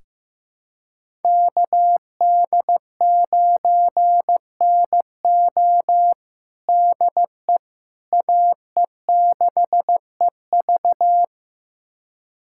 Morse code, heard as a keyed tone, about 15 wpm, 700 Hz. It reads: KD9NO DE AE6EV
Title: USA Callsign DE Callsign - 5 Characters - 15wpm